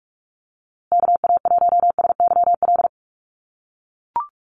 Morse code, heard as a keyed tone, about 44 wpm, 700 Hz. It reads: KA1SYL
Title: USA Call Signs Repeated 3x 45wpm